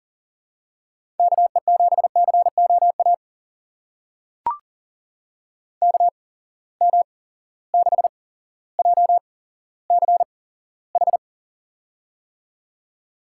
Morse code, heard as a keyed tone, about 40 wpm, spaced as for 17 wpm, 700 Hz. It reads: KE7COA KM6JCH
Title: USA Call Signs - 6 Characters - 8x Character Spacing - 40wpm